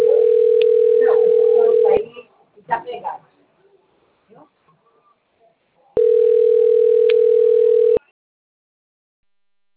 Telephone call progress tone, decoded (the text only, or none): ringback tone